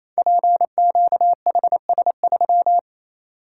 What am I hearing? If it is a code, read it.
Morse code, heard as a keyed tone, about 28 wpm, 700 Hz. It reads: PQHS3
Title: Code Groups - 5 Letter:Numbers - 28wpm